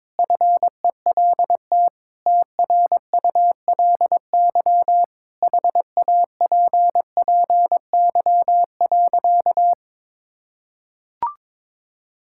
Morse code, heard as a keyed tone, about 22 wpm, 700 Hz. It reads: FELT TRULY HAPPY.
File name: Sentences with 7-Word Phrases - 22wpm